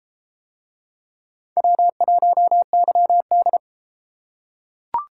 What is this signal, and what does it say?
Morse code, heard as a keyed tone, about 31 wpm, 700 Hz. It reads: W1YD